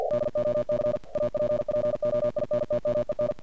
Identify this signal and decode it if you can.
Morse code, heard as a keyed tone, about 50 wpm, 600 Hz, with noise. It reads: DOC APPOINTMEN
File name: archived / CRSnet